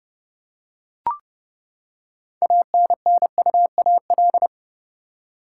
Morse code, heard as a keyed tone, about 30 wpm, 700 Hz. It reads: ANNUAL
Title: Six Letter Words - 30wpm